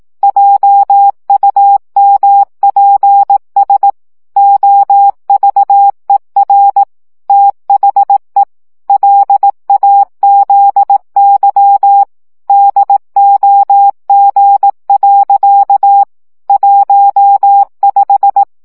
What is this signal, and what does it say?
Morse code, heard as a keyed tone, about 18 wpm, 800 Hz. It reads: JUMPS OVER THE LAZY DOG. 15
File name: CW 15 baud